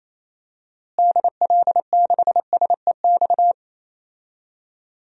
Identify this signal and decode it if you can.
Morse code, heard as a keyed tone, about 28 wpm, 700 Hz. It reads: DL6SEX
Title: DXCC-100 Call Signs - 6 Characters - Repeated 2x - 28wpm